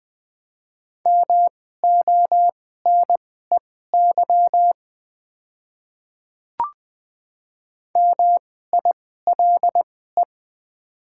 Morse code, heard as a keyed tone, about 20 wpm, spaced as for 15 wpm, 690 Hz. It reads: MONEY MILE
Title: 2x Character Spacing - Top 500 Words - 20wpm